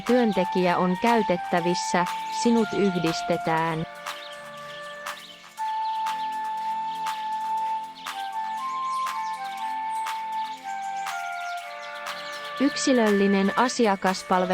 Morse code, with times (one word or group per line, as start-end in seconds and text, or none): none